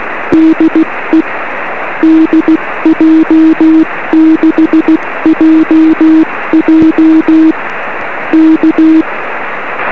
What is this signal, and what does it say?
Morse code, heard as a keyed tone, about 16 wpm, 320 Hz, with noise. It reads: DE DJ6JJ K
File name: 1911z dj6ff 13cm